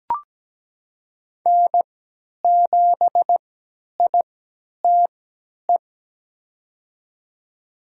Morse code, heard as a keyed tone, about 17 wpm, spaced as for 10 wpm, 700 Hz. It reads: N7ITE
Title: USA Call Signs - 5 Characters - 3x Character Spacing - 17wpm